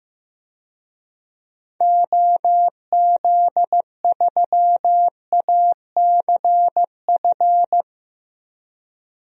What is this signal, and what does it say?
Morse code, heard as a keyed tone, about 15 wpm, 700 Hz. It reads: OZ3ACF
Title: DXCC-100 Call Signs - 6 Characters - 15wpm